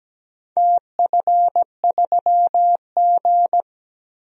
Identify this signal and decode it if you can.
Morse code, heard as a keyed tone, about 17 wpm, 700 Hz. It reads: TF3G
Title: DXCC-100 Call Signs - 4 Characters - Repeated 2x - 17wpm